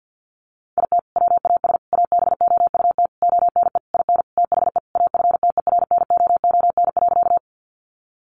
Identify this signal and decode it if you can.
Morse code, heard as a keyed tone, about 50 wpm, 700 Hz. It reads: IT WAS ABOUT ONE IN THE AFTERNOON.
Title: Sentences from Top 1500 Words 50wpm